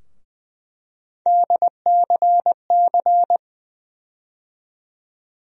Morse code, heard as a keyed tone, about 21 wpm, 700 Hz. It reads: DCC
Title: USA Call Sign Postfixes - 3 Letters 20wpm